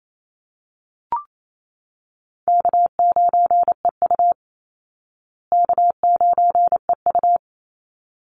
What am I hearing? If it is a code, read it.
Morse code, heard as a keyed tone, about 28 wpm, 700 Hz. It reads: K9EU K9EU
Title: Call Signs Repeated 3x - 28wpm